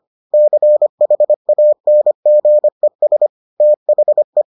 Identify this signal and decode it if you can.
Morse code, heard as a keyed tone, about 25 wpm, 600 Hz. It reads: CHANGES THE